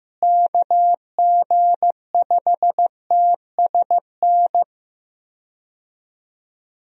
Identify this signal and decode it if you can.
Morse code, heard as a keyed tone, about 15 wpm, 700 Hz. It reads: KG5TSN
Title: USA Call Signs - 6 Characters - Repeated 2x - 15wpm